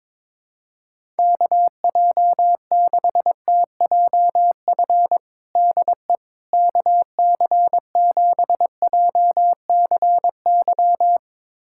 Morse code, heard as a keyed tone, about 22 wpm, 700 Hz. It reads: KJ6TJF DE KC7JCY
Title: USA Callsign DE Callsign - 6 Characters - 22wpm